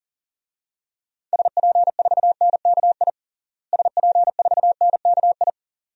Morse code, heard as a keyed tone, about 40 wpm, 700 Hz. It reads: SP4NKI SP4NKI
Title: DXCC-100 Call Signs - 6 Characters - Repeated 2x - 40wpm